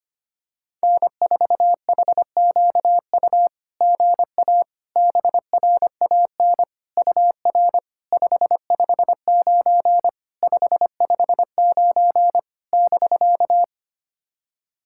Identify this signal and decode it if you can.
Morse code, heard as a keyed tone, about 25 wpm, 700 Hz. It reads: N4HQU GA BRAN UR 559 559 <BK>
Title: SOTA QSO Practice 25wpm